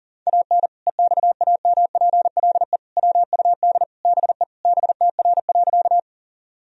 Morse code, heard as a keyed tone, about 40 wpm, 700 Hz. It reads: AN EXAMPLE WUD BE BTR.
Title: Sentences from Top 400 Words and Abbreviations 40wpm